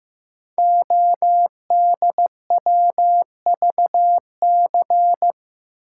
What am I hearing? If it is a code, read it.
Morse code, heard as a keyed tone, about 15 wpm, 690 Hz. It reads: ODWVC
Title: Code Groups - 5 Letters - 15wpm